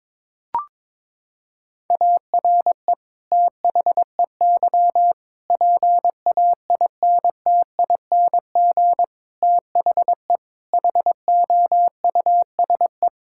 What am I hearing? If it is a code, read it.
Morse code, heard as a keyed tone, about 22 wpm, 700 Hz. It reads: ARE THEY PAINTING THE HOUSE
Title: Sentences from Top 1000 Words - Review Part 6 - 22wpm